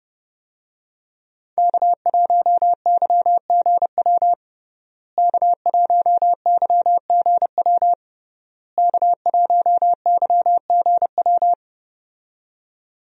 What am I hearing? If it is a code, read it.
Morse code, heard as a keyed tone, about 32 wpm, 700 Hz. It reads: K1YGW K1YGW K1YGW
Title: USA Call Signs - 5 Characters - Repeated 3x - 30wpm